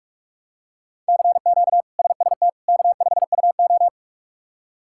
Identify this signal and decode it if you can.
Morse code, heard as a keyed tone, about 46 wpm, 700 Hz. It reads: CQ SST K5UO
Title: K1USN SST QSO Practice - 45wpm